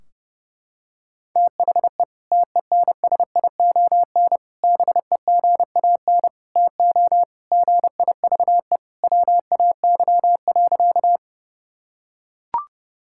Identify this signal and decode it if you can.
Morse code, heard as a keyed tone, about 30 wpm, 700 Hz. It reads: THE TENSION BEGAN TO GIVE WAY.
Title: Sentences from Top 1600 Words 30wpm